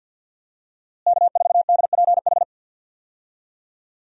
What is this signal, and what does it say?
Morse code, heard as a keyed tone, about 52 wpm, 700 Hz. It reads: K4DPH
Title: USA Call Signs - 5 Characters - Repeated 3x - 50wpm